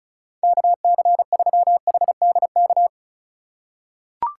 Morse code, heard as a keyed tone, about 36 wpm, 700 Hz. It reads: KC3HDK
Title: USA Call Signs - 6 Characters - 2x Character Spacing - 35wpm